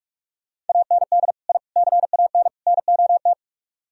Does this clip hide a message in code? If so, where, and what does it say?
Yes: Morse code from 0.7 to 3.3 s, heard as a keyed tone, about 44 wpm, 700 Hz. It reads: AND I CAN NOT